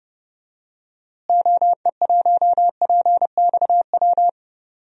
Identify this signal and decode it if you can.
Morse code, heard as a keyed tone, about 30 wpm, 700 Hz. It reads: OE1PXW
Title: DXCC-100 Call Signs - 6 Characters - Repeated 2x - 30wpm